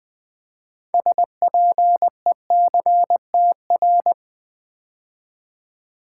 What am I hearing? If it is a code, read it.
Morse code, heard as a keyed tone, about 20 wpm, 700 Hz. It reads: SPECTR
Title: Build A Word - 2 Letters At A Time - 8-Letter Words - 20wpm